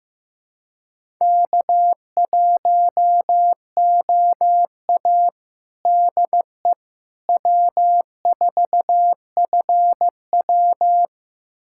Morse code, heard as a keyed tone, about 15 wpm, 700 Hz. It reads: K1OA DE W4FW
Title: USA Callsign DE Callsign - 4 Characters - 15wpm